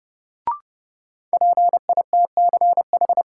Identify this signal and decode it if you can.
Morse code, heard as a keyed tone, about 32 wpm, 700 Hz. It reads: PITCH
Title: Sentences from Top 1300 Words - New Word Drill - 30wpm